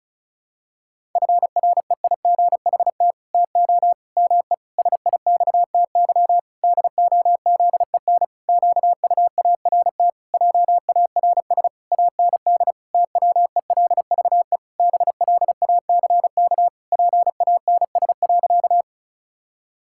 Morse code, heard as a keyed tone, about 36 wpm, 700 Hz. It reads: FREIGHT TO ME SIXTY DOZEN QUART JARS AND TWELVE BLACK PANS.
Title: Pangram Challenge v2 - 35wpm